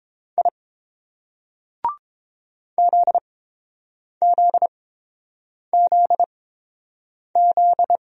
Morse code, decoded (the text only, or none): I ZZZZ